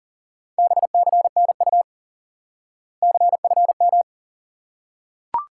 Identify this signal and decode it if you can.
Morse code, heard as a keyed tone, about 40 wpm, 700 Hz. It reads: BCNU CFM
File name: IQE Course 3 - Lesson 89 - YAGI - Two Elements - 40wpm